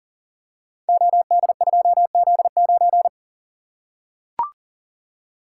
Morse code, heard as a keyed tone, about 42 wpm, 690 Hz. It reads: OD2Z9
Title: Code Groups - 5 Letter:Numbers - 40wpm